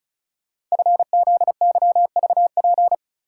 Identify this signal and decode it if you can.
Morse code, heard as a keyed tone, about 36 wpm, 690 Hz. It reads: FZYVP
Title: Code Groups - 5 Letter:Numbers - 35wpm